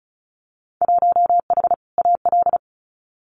Morse code, heard as a keyed tone, about 36 wpm, 700 Hz. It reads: 1H AL
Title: Winter Field Day Exchanges 35wpm